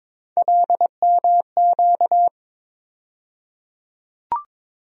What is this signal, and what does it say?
Morse code, heard as a keyed tone, about 22 wpm, 700 Hz. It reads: LMQ
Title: USA Call Sign Postfixes - 3 Letters 22wpm